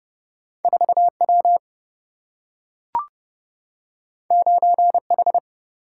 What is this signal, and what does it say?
Morse code, heard as a keyed tone, about 31 wpm, 700 Hz. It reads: 4W 9H
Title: Single-Letter Single-Number 30wpm